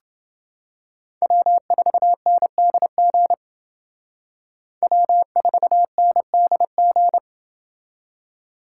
Morse code, heard as a keyed tone, about 29 wpm, 700 Hz. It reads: W4NDG W4NDG